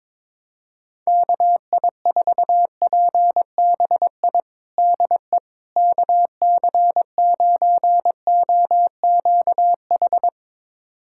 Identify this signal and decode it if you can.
Morse code, heard as a keyed tone, about 22 wpm, 700 Hz. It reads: KI4PBI DE KC9OQH
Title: USA Callsign DE Callsign 22wpm